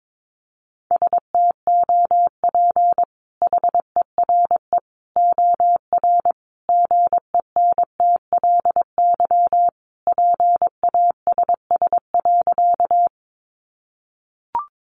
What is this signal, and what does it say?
Morse code, heard as a keyed tone, about 22 wpm, 700 Hz. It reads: STOP HERE OR GENTLY PASS.
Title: Sentences from Top 1500 Words 22wpm